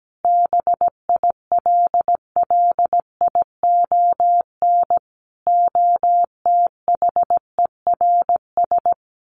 Morse code, heard as a keyed tone, about 17 wpm, 700 Hz. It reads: BILLION OTHERS
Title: A Sound Of Thunder 17wpm